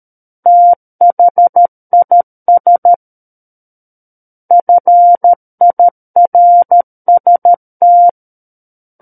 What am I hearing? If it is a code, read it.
Morse code, heard as a keyed tone, about 13 wpm, 700 Hz. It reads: THIS FIRST
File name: Book One / 04-Chapter 4 - 13 WPM